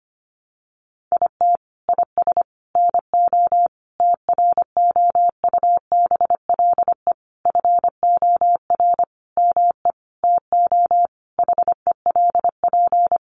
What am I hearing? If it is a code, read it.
Morse code, heard as a keyed tone, about 25 wpm, 700 Hz. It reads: IT IS NO TROUBLE FOR ME TO HELP
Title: Sentences from Top 1200 Words 25wpm